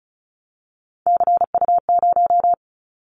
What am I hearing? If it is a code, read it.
Morse code, heard as a keyed tone, about 34 wpm, 700 Hz. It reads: CU0
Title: Callsign Prefixes for Top 50 DXCC - 35wpm